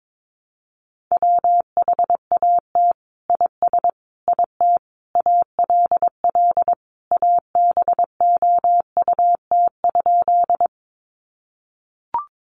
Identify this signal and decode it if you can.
Morse code, heard as a keyed tone, about 22 wpm, 700 Hz. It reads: WHAT IS IT ALL ABOUT?
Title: Sentences from Top 100 Words 22wpm